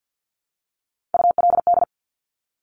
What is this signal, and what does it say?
Morse code, heard as a keyed tone, about 50 wpm, 700 Hz. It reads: ULD
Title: Top Three Letter Combinations in Words 50wpm